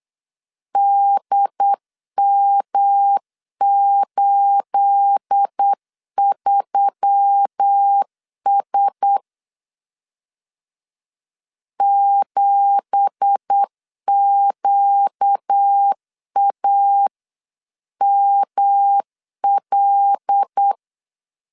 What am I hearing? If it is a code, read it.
Morse code, heard as a keyed tone, about 8 wpm, 800 Hz. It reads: DM83S 7QAML
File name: CW / DET / 1 9 1500C S B 8wpm 5624